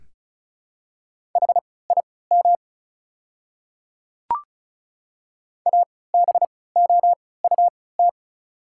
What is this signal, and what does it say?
Morse code, heard as a keyed tone, about 36 wpm, spaced as for 20 wpm, 690 Hz. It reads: HIM ABOUT
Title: 3x Character Spacing - Top 100 Words - 35wpm